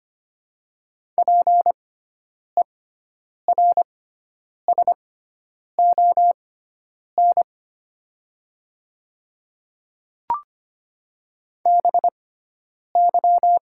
Morse code, heard as a keyed tone, about 25 wpm, spaced as for 9 wpm, 700 Hz. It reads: PERSON BY
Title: 6x Character Spacing - Top 500 Words - 25wpm